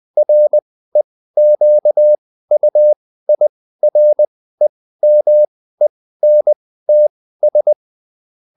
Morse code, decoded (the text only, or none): REQUIREMENTS